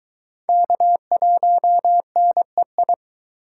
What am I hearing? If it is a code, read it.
Morse code, heard as a keyed tone, about 22 wpm, 700 Hz. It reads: K1NEI